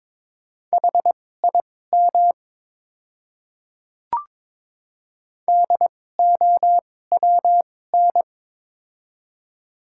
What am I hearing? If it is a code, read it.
Morse code, heard as a keyed tone, about 22 wpm, spaced as for 16 wpm, 700 Hz. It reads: HIM DOWN
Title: 2x Character Spacing - Top 100 Words - 22wpm